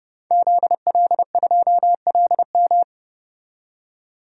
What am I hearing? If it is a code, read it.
Morse code, heard as a keyed tone, about 30 wpm, 700 Hz. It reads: ZL2LM
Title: DXCC-100 Call Signs - 5 Characters - 30wpm